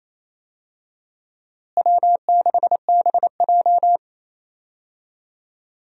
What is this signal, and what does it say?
Morse code, heard as a keyed tone, about 28 wpm, 700 Hz. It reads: W6BJ
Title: USA Call Signs - 4 Characters - Repeated 2x - 28wpm